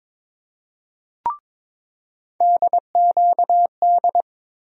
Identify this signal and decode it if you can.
Morse code, heard as a keyed tone, about 22 wpm, 700 Hz. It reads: DQD
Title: USA Call Sign Postfixes - 3 Letters 22wpm